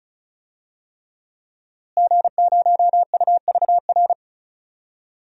Morse code, heard as a keyed tone, about 34 wpm, 700 Hz. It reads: G0UVR
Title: DXCC-100 Call Signs - 5 Characters - Repeated 3x - 35wpm